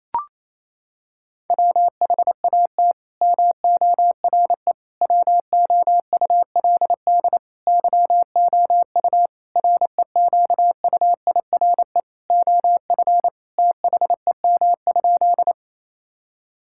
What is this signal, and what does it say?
Morse code, heard as a keyed tone, about 28 wpm, 700 Hz. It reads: WHAT MORE WOULD YOU REQUIRE OF THEM?
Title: Sentences from Top 1000 Words - Review Part 6 - 28wpm